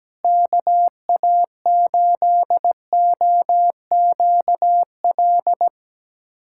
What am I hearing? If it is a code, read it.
Morse code, heard as a keyed tone, about 17 wpm, 700 Hz. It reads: KA8OQL
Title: USA Callsign DE Callsign 17wpm